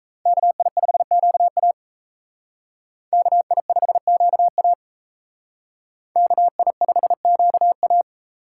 Morse code, heard as a keyed tone, about 32 wpm, 700 Hz. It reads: KI5QA KI5QA KI5QA
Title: Speed Racing - USA Call Signs 28wpm